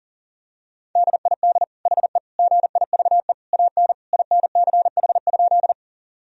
Morse code, heard as a keyed tone, about 40 wpm, 700 Hz. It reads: DID HE GIVE AN INCH?
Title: Sentences from Top 1000 Words - Review Part 3 - 40wpm